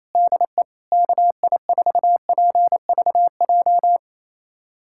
Morse code, heard as a keyed tone, about 28 wpm, 700 Hz. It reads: DE KI4PVJ
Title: USA Callsign DE Callsign - 6 Characters - 28wpm